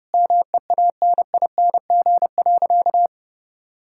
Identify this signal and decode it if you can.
Morse code, heard as a keyed tone, about 30 wpm, 700 Hz. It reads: MEANING.
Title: Sentences from Top 1000 Words - Review Part 6 - 30wpm